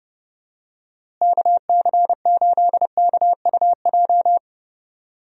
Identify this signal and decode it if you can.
Morse code, heard as a keyed tone, about 32 wpm, 700 Hz. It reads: KC8KUJ
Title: USA Call Signs - 6 Characters - Repeated 3x - 30wpm